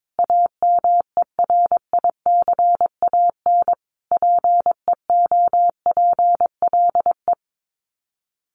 Morse code, heard as a keyed tone, about 22 wpm, 700 Hz. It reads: AMERICAN PEOPLE
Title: Sets of 5 Words Encore 22wpm